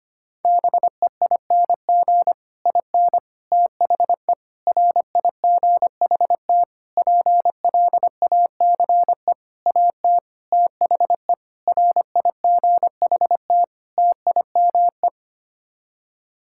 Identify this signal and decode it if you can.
Morse code, heard as a keyed tone, about 25 wpm, 700 Hz. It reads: BEING IN THE RIGHT PLACE AT THE RIGHT TIME
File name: Sets of 9 Words - 25wpm